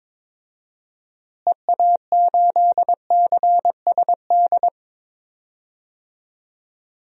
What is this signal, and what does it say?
Morse code, heard as a keyed tone, about 22 wpm, 700 Hz. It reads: EA8CSD
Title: DXCC-100 Call Signs - 6 Characters - Repeated 3x - 22wpm